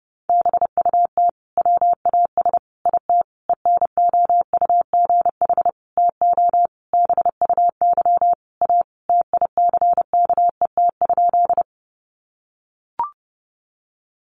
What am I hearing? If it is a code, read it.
Morse code, heard as a keyed tone, about 30 wpm, 700 Hz. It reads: BUT WAS IT ENOUGH TO BUY A TICKET?
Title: Sentences from Top 1400 Words 30wpm